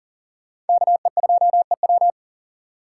Morse code, heard as a keyed tone, about 40 wpm, 700 Hz. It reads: KE2EW
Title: USA Call Signs - 2x Character Spacing - 40wpm